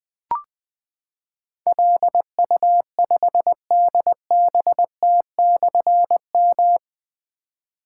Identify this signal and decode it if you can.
Morse code, heard as a keyed tone, about 20 wpm, 700 Hz. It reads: LU5DBT/M
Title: Difficult Call Signs 20wpm